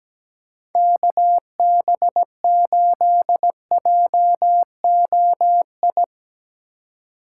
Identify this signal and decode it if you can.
Morse code, heard as a keyed tone, about 17 wpm, 700 Hz. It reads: KB8JOI
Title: USA Call Signs - 6 Characters - 2x Character Spacing - 17wpm